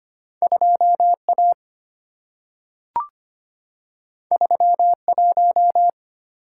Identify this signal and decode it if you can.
Morse code, heard as a keyed tone, about 25 wpm, 690 Hz. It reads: 2A 31
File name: Two Letter-Number Combinations - 25wpm